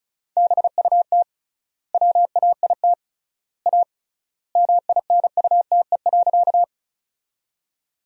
Morse code, heard as a keyed tone, about 34 wpm, 700 Hz. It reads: BUT WAIT A MINUTE.
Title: 3x Word-Spacing - Sentences from Top 500 Words - 35wpm